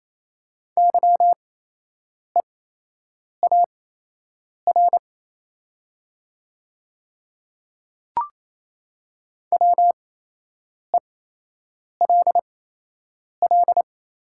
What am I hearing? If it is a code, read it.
Morse code, heard as a keyed tone, about 27 wpm, spaced as for 8 wpm, 690 Hz. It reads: YEAR WELL
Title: 8x Character Spacing - Top 300 Words - 28wpm